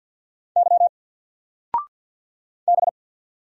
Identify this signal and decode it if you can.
Morse code, heard as a keyed tone, about 50 wpm, 700 Hz. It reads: Y B
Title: Single Letter-Number 50wpm